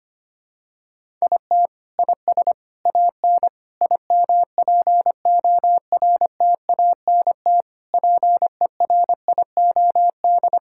Morse code, heard as a keyed tone, about 25 wpm, 700 Hz. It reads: IT IS AN IMPORTANT PERIOD
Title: Sentences from Top 1000 Words - Review Part 5 - 25wpm